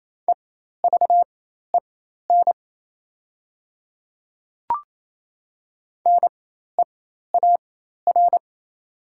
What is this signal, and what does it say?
Morse code, heard as a keyed tone, about 28 wpm, spaced as for 13 wpm, 690 Hz. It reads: EVEN NEAR
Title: 4x Character Spacing - Top 300 Words - 28wpm